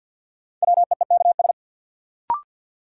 Morse code, heard as a keyed tone, about 48 wpm, 700 Hz. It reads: WEEKS
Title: Sets of 5 Words - New Word Drill - 50wpm